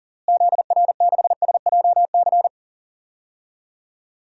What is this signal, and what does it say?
Morse code, heard as a keyed tone, about 40 wpm, 700 Hz. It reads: ZR6SJC
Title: DXCC-100 Call Signs - 6 Characters - Repeated 3x - 40wpm